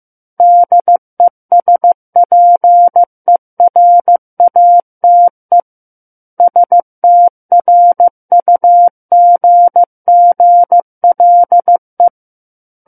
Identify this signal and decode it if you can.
Morse code, heard as a keyed tone, about 15 wpm, 700 Hz. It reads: DESPERATE STRUGGLE
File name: Book One / 06-Chapter 6 - 15 WPM